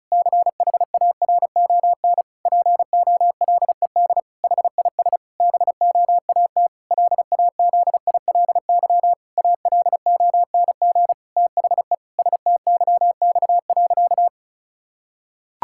Morse code, heard as a keyed tone, about 36 wpm, 700 Hz. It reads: CHARON POLED HIS BOAT LAZILY ALONG THE STYX.
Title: Pangram Challenge v2 - 35wpm